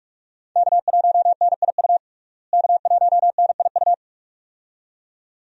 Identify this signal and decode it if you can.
Morse code, heard as a keyed tone, about 46 wpm, 700 Hz. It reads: K1NIU K1NIU
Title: USA Call Signs - 5 Characters - Repeated 2x - 45wpm